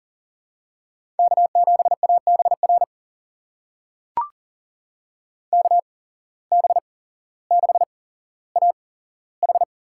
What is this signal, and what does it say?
Morse code, heard as a keyed tone, about 40 wpm, spaced as for 17 wpm, 700 Hz. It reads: K7ABR KB6AH
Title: USA Call Signs - 8x Character Spacing - 40wpm